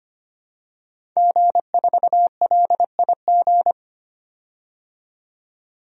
Morse code, heard as a keyed tone, about 25 wpm, 700 Hz. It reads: G4LIG
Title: DXCC-100 Call Signs - 5 Characters - 25wpm